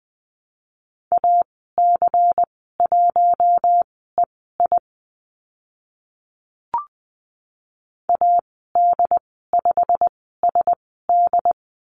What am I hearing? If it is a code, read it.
Morse code, heard as a keyed tone, about 20 wpm, spaced as for 15 wpm, 700 Hz. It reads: AC1EI AD5SD
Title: 140 Lesson 23g - Letter f - Call Signs 20wpm